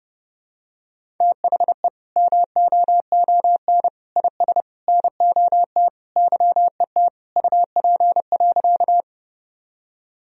Morse code, heard as a keyed tone, about 32 wpm, 700 Hz. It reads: THE MOON IS NOT YET UP.
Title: Sentences from Top 1000 Words - Review Part 3 - 30wpm